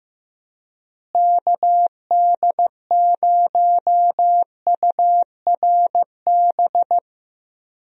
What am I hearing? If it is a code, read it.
Morse code, heard as a keyed tone, about 15 wpm, 700 Hz. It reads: KD0URB
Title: USA Call Signs - 6 Characters - Repeated 3x - 15wpm